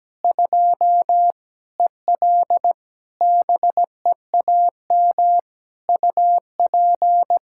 Morse code, heard as a keyed tone, about 17 wpm, 700 Hz. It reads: 2 EL BEAM UP
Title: Practice QSO - Exchange 2 - Info - 17wpm